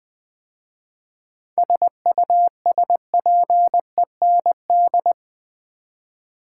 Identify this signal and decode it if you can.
Morse code, heard as a keyed tone, about 20 wpm, 700 Hz. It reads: SUSPEND
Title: 7 Letter Words - First 3 Letters Given - 20wpm